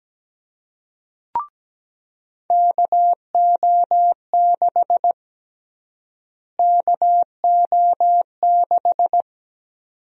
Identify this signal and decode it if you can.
Morse code, heard as a keyed tone, about 17 wpm, 700 Hz. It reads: KO6 KO6
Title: USA Call Sign Prefixes - Repeated 2x - 17wpm